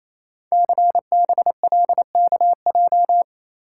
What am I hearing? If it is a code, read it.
Morse code, heard as a keyed tone, about 29 wpm, 700 Hz. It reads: CBLKJ
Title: Code Groups - 5 Letters - 28wpm